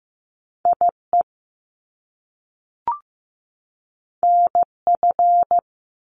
Morse code, heard as a keyed tone, about 15 wpm, 700 Hz. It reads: IE NF
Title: Top Two Letter Combinations in Words 15wpm